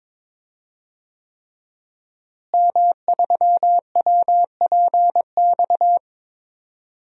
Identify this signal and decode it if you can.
Morse code, heard as a keyed tone, about 22 wpm, 700 Hz. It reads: M3WPX
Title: DXCC-100 Call Signs - 5 Characters - Repeated 3x - 22wpm